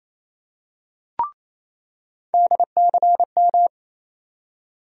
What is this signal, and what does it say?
Morse code, heard as a keyed tone, about 27 wpm, 700 Hz. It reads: DCM
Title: USA Call Sign Postfixes - 3 Letters 28wpm